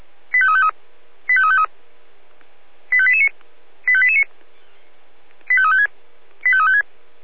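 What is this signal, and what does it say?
Five-tone selective call (ZVEI), heard as quick runs of five tones, tones of about 70 ms. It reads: 85353 85353 85889 85889 85356 85356
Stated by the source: ZVEI five-tone selective call received on an SDR.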